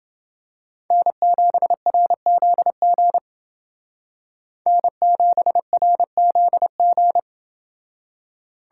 Morse code, heard as a keyed tone, about 28 wpm, 700 Hz. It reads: N7RZG N7RZG